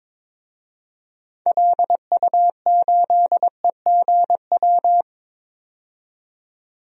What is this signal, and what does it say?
Morse code, heard as a keyed tone, about 22 wpm, 700 Hz. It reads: LU8EGW
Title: DXCC-100 Call Signs - 6 Characters - 22wpm